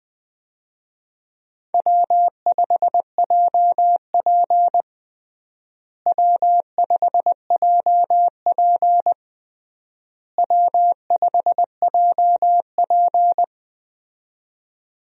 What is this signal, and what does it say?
Morse code, heard as a keyed tone, about 20 wpm, 700 Hz. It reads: W5JP W5JP W5JP
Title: USA Call Signs - 4 Characters - Repeated 3x - 20wpm